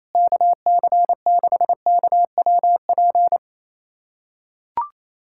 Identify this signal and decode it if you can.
Morse code, heard as a keyed tone, about 28 wpm, 700 Hz. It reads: KC6KWP
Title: USA Callsign DE Callsign - 6 Characters - 28wpm